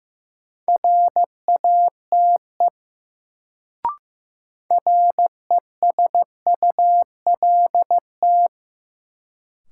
Morse code, heard as a keyed tone, about 15 wpm, 700 Hz. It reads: RATE RESULT
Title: Sentences from Top 700 Words - New Word Drill - 15wpm